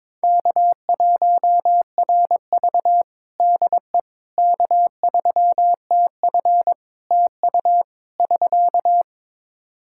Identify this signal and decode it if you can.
Morse code, heard as a keyed tone, about 22 wpm, 700 Hz. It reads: K1RV DE K3TF TU <SK>
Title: Practice QSO - Exchange 4 - Ending - 22wpm